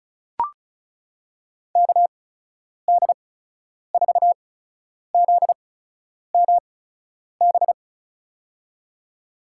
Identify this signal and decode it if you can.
Morse code, heard as a keyed tone, about 36 wpm, spaced as for 10 wpm, 700 Hz. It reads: KD4ZMB
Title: USA Call Signs - 6 Characters - 8x Character Spacing - 35wpm